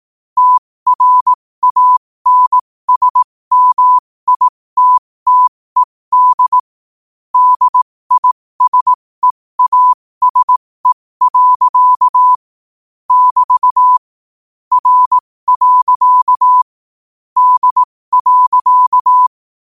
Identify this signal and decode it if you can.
Morse code, heard as a keyed tone, about 18 wpm, spaced as for 15 wpm, 1000 Hz. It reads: TRANSMITTED DISEASE. <BT> R. D.
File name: qotdfiles / 15wpm